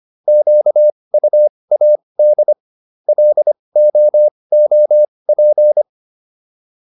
Morse code, heard as a keyed tone, about 25 wpm, spaced as for 20 wpm, 600 Hz. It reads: QUAD LOOP